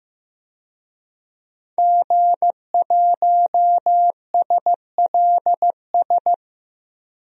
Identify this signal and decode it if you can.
Morse code, heard as a keyed tone, about 15 wpm, 700 Hz. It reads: G1SLS
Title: DXCC-100 Call Signs - 5 Characters - Repeated 2x - 15wpm